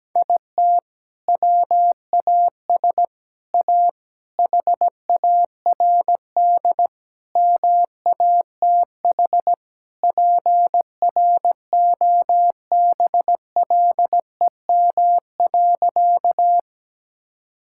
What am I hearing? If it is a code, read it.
Morse code, heard as a keyed tone, about 17 wpm, 700 Hz. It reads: IT WAS A HARD MATH PROBLEM.
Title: Sentences from Top 1000 Words - Review Part 6 - 17wpm